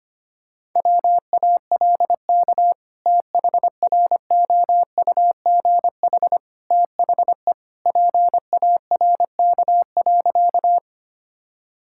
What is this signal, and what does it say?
Morse code, heard as a keyed tone, about 25 wpm, 700 Hz. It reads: WALK THROUGH THE PARK.
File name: practice / 25wpm / Sentences with 2-Word Phrases - 25wpm